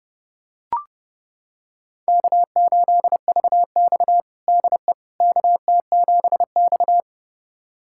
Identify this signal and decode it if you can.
Morse code, heard as a keyed tone, about 31 wpm, 700 Hz. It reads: K8VX DE KT7X
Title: USA Callsign DE Callsign - 4 Characters - 30wpm